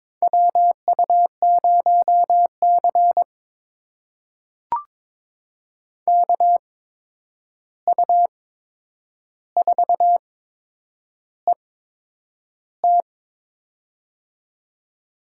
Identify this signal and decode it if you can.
Morse code, heard as a keyed tone, about 22 wpm, spaced as for 9 wpm, 700 Hz. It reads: WU0C KU4ET
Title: USA Call Signs - 8x Character Spacing - 22wpm